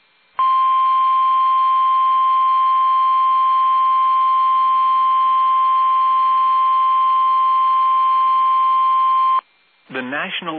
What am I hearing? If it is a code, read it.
Emergency Alert System (SAME) tones (the 1050 Hz attention tone), with speech. It is heard from 0.4 to 9.4 s.